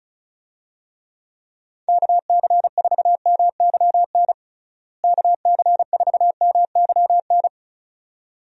Morse code, heard as a keyed tone, about 34 wpm, 700 Hz. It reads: KC4MYN KC4MYN
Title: USA Call Signs - 6 Characters - Repeated 2x - 35wpm